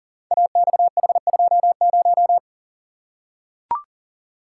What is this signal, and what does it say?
Morse code, heard as a keyed tone, about 42 wpm, 700 Hz. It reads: AXH20
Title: Code Groups - 5 Letter:Numbers - 40wpm